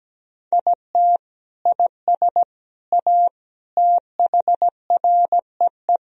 Morse code, heard as a keyed tone, about 17 wpm, 700 Hz. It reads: IT IS A THREE
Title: Sentences from Top 1000 Words - Review Part 1 - 17wpm